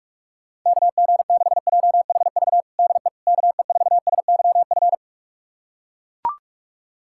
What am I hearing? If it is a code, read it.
Morse code, heard as a keyed tone, about 41 wpm, 700 Hz. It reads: KG6JHV DE KE4SYF
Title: USA Callsign DE Callsign - 6 Characters - 45wpm